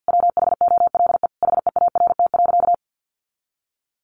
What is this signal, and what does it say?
Morse code, heard as a keyed tone, about 50 wpm, 700 Hz. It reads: WHOLE HEART.
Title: Sentences from Top 1200 Words 50wpm